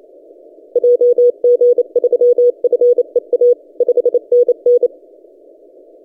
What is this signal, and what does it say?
Morse code, heard as a keyed tone, about 28 wpm, 490 Hz. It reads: JG3FEA 5NN
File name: t30d-12cw